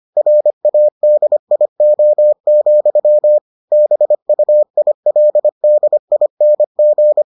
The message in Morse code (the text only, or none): RADIO, BUILDING